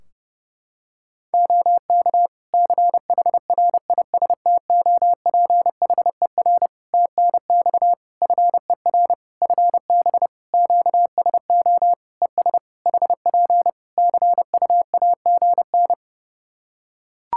Morse code, heard as a keyed tone, about 30 wpm, 700 Hz. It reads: OK CHRISTOPHER TNX FER FB QSO ES HP CUAGN
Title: Practice QSO - Exchange 4 - Ending - 30wpm